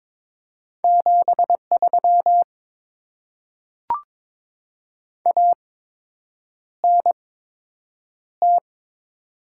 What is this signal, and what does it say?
Morse code, heard as a keyed tone, about 22 wpm, spaced as for 8 wpm, 700 Hz. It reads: 73 ANT
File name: Top 20 QSO Elements - 8x Character Spacing - 22wpm